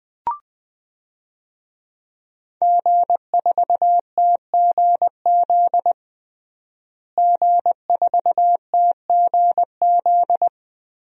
Morse code, heard as a keyed tone, about 20 wpm, 700 Hz. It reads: G4TGZ G4TGZ
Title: DXCC-100 Call Signs - 5 Characters - Repeated 2x - 20wpm